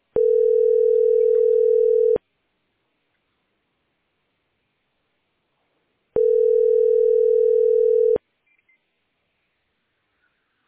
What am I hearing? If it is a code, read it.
Ringback tone (North America), 440 + 480 Hz.